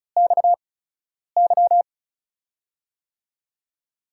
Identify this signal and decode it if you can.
Morse code, heard as a keyed tone, about 36 wpm, spaced as for 10 wpm, 700 Hz. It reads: XY
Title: USA Call Signs - 8x Character Spacing - 35wpm